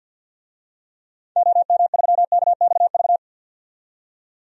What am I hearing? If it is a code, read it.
Morse code, heard as a keyed tone, about 52 wpm, 700 Hz. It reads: OM3KXV
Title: DXCC-100 Call Signs - 6 Characters - 50wpm